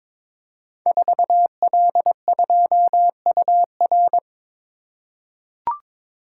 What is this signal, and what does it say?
Morse code, heard as a keyed tone, about 23 wpm, 700 Hz. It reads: 4L2UR
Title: Code Groups - 5 Letter:Numbers - 22wpm